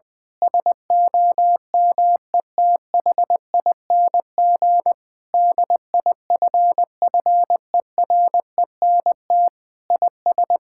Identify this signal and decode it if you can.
Morse code, heard as a keyed tone, about 21 wpm, 700 Hz. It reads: SOMETHING DIFFERENT IS
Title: Sentences with 8-Word Phrases - 20wpm